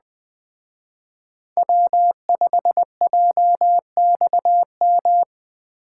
Morse code, heard as a keyed tone, about 20 wpm, 700 Hz. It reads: W5JXM